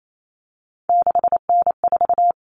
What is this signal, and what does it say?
Morse code, heard as a keyed tone, about 28 wpm, 700 Hz. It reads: 6N4
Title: Common Cut Numbers - 3-Digit 28wpm